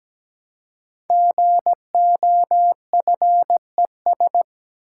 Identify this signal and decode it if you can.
Morse code, heard as a keyed tone, about 17 wpm, 700 Hz. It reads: GOFES
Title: Code Groups - 5 Letters - 17wpm